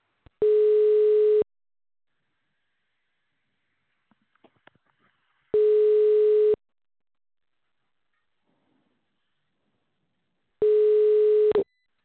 Ringback tone (Europe, 425 Hz).